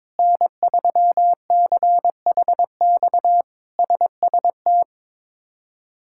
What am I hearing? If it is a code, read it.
Morse code, heard as a keyed tone, about 22 wpm, 700 Hz. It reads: N3CHX SST
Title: K1USN SST QSO Practice - 22wpm